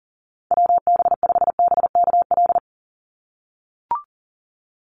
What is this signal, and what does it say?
Morse code, heard as a keyed tone, about 40 wpm, 700 Hz. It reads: WB5BKL
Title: NA SOTA Chaser Callsigns 40wpm